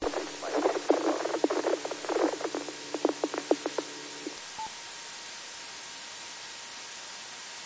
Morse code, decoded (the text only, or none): none